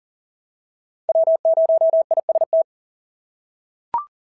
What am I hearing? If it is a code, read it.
Morse code, heard as a keyed tone, about 39 wpm, 650 Hz. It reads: W0IST
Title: SOTA QSO Practice 40wpm